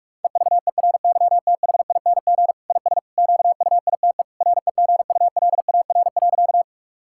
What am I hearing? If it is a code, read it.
Morse code, heard as a keyed tone, about 46 wpm, 700 Hz. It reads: EVERYTHING IS QUITE REGULAR.
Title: Sentences from Top 1000 Words - Review Part 6 - 45wpm